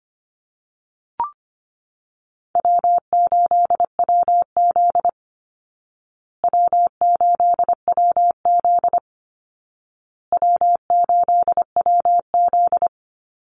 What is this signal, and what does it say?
Morse code, heard as a keyed tone, about 25 wpm, 700 Hz. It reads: W8WZ W8WZ W8WZ
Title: Easy Call Signs Repeated 3x - 25wpm